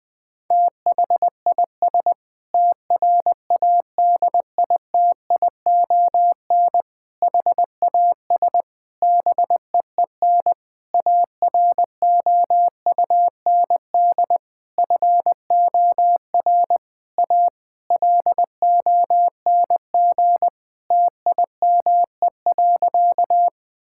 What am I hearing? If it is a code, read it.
Morse code, heard as a keyed tone, about 20 wpm, 700 Hz. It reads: THIS TRADITION HAS BEEN AROUND FOR A LONG TIME.
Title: Sentences with 7-Word Phrases - 20wpm